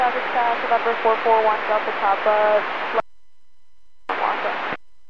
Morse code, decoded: none

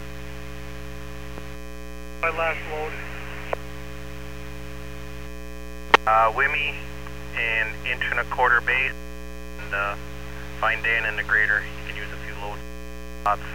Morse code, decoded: none

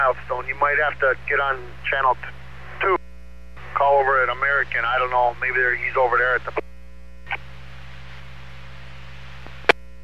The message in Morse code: none

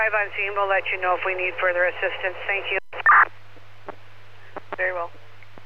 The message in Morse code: none

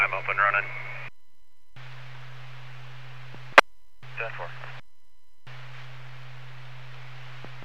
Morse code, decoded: none

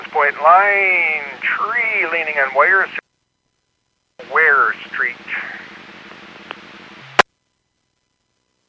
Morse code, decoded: none

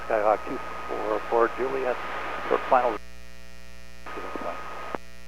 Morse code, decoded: none

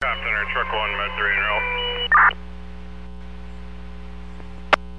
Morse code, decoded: none

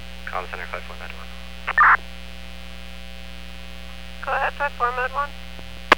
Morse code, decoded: none

